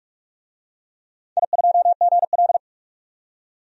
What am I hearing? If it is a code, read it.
Morse code, heard as a keyed tone, about 44 wpm, 700 Hz. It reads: I2GL
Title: DXCC-100 Call Signs - 4 Characters - 45wpm